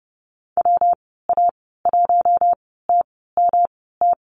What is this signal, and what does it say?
Morse code, heard as a keyed tone, about 30 wpm, spaced as for 17 wpm, 700 Hz. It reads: WA1TMT
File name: Lesson 19g - Letter w - Call Signs 30wpm